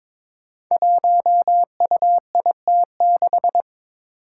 Morse code, heard as a keyed tone, about 22 wpm, 700 Hz. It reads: 1UIT6
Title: Five Challenging Letter-Number Combinations 22wpm